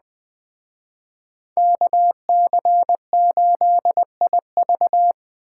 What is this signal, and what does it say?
Morse code, heard as a keyed tone, about 20 wpm, 700 Hz. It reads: KC8IV